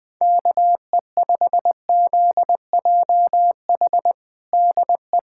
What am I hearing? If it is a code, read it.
Morse code, heard as a keyed tone, about 20 wpm, 700 Hz. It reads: KE5ZJH DE
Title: USA Callsign DE Callsign - 6 Characters - 20wpm